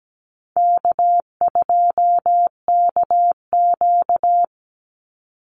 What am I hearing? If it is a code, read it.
Morse code, heard as a keyed tone, about 17 wpm, 700 Hz. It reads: K2KQ
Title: Easy Call Signs Repeated 2x - 17wpm